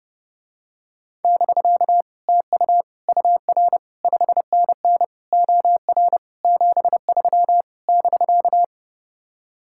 Morse code, heard as a keyed tone, about 30 wpm, 700 Hz. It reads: <BK> TU UR 5NN OR 73 <BK>